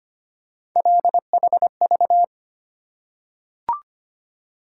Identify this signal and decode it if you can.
Morse code, heard as a keyed tone, about 25 wpm, 700 Hz. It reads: LHV
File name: Three Letter-Number Combinations - 25wpm